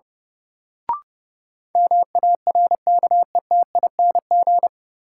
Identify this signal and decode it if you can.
Morse code, heard as a keyed tone, about 32 wpm, 700 Hz. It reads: MARKETING
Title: Sentences from Top 1000 Words - Review Part 5 - New Word Drill - 30wpm